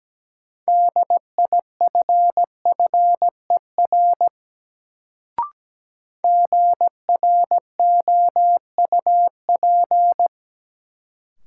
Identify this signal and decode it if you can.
Morse code, heard as a keyed tone, about 17 wpm, 700 Hz. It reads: DIFFER GROUP
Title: Sentences from Top 1000 Words - Review Part 1 - New Word Drill - 17wpm